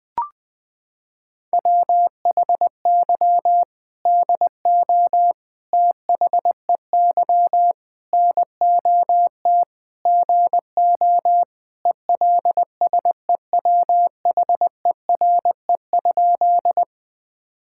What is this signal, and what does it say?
Morse code, heard as a keyed tone, about 20 wpm, 700 Hz. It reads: WHY DO THEY NOT GO ELSEWHERE?
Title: Sentences from Top 1600 Words 20wpm